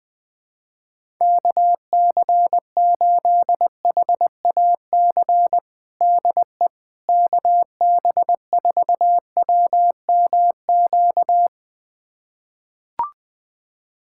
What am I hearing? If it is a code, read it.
Morse code, heard as a keyed tone, about 20 wpm, 700 Hz. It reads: KC8HAC DE KB4WMQ
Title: USA Callsign DE Callsign - 6 Characters - 20wpm